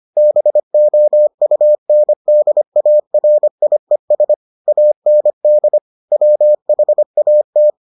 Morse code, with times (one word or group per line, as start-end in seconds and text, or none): 0.2-4.3 BOUNDARIES
4.7-5.8 AND
6.1-7.7 WHAT